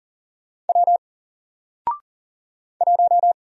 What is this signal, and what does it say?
Morse code, heard as a keyed tone, about 40 wpm, 700 Hz. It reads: W 1
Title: Single Character (Letter:Number:Punctuation:Prosign) Alt 40wpm